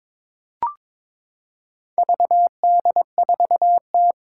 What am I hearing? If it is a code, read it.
Morse code, heard as a keyed tone, about 22 wpm, 700 Hz. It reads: VD4T
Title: Four Challenging Letter-Number Combinations 22wpm